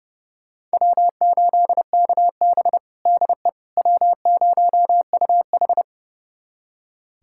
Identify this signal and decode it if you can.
Morse code, heard as a keyed tone, about 30 wpm, 700 Hz. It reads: W8KB DE W0UH
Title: USA Callsign DE Callsign - 4 Characters - 30wpm